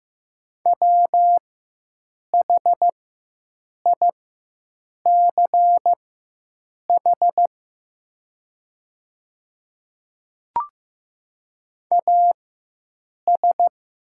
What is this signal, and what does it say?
Morse code, heard as a keyed tone, about 15 wpm, spaced as for 7 wpm, 700 Hz. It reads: WHICH AS
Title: 4x Character Spacing - Top 100 Words - 15wpm